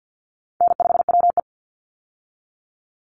Morse code, heard as a keyed tone, about 50 wpm, 700 Hz. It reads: N5WE
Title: Call Signs - 50wpm